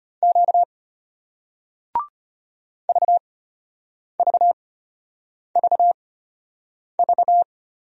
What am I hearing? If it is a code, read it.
Morse code, heard as a keyed tone, about 33 wpm, spaced as for 8 wpm, 700 Hz. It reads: Q VVVV